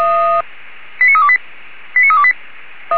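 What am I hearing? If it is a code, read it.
Five-tone selective call (ZVEI), heard as quick runs of five tones, tones of about 70 ms. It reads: 89328 89328
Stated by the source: ZVEI five-tone selective call received on an SDR.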